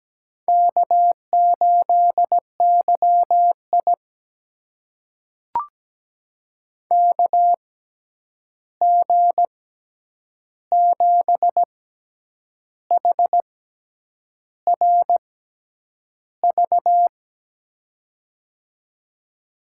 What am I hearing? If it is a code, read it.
Morse code, heard as a keyed tone, about 17 wpm, spaced as for 8 wpm, 700 Hz. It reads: K8YI KG7HRV
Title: USA Call Signs - 6x Character Spacing - 17wpm